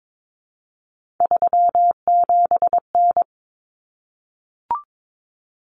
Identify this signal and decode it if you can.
Morse code, heard as a keyed tone, about 22 wpm, 700 Hz. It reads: 37N
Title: Common Cut Numbers - 3-Digit 22wpm